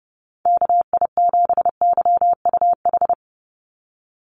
Morse code, heard as a keyed tone, about 32 wpm, 700 Hz. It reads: KI7YUH
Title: USA Call Signs 30wpm